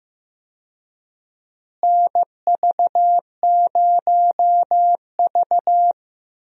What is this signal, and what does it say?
Morse code, heard as a keyed tone, about 15 wpm, 700 Hz. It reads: NV0V
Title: USA Call Signs - 4 Characters - Repeated 2x - 15wpm